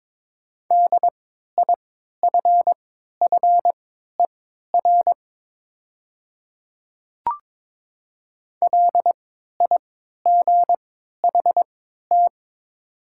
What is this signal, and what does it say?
Morse code, heard as a keyed tone, about 22 wpm, spaced as for 12 wpm, 700 Hz. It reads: DIFFER LIGHT
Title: 3x Character Spacing - Top 200 Words - 22wpm